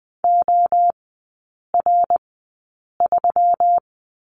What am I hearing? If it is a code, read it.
Morse code, heard as a keyed tone, about 20 wpm, spaced as for 8 wpm, 700 Hz. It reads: OR3
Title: 133 Lesson 22e - Question Mark - Three Characters 20wpm